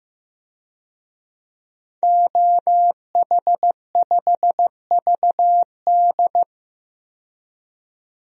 Morse code, heard as a keyed tone, about 15 wpm, 700 Hz. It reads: OH5VD
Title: DXCC-100 Call Signs - 5 Characters - 15wpm